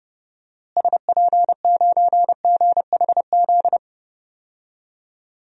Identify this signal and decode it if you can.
Morse code, heard as a keyed tone, about 30 wpm, 700 Hz. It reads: SP9GHZ
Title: DXCC-100 Call Signs - 6 Characters - Repeated 2x - 30wpm